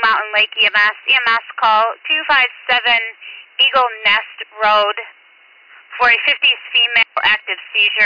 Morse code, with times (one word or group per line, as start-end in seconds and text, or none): none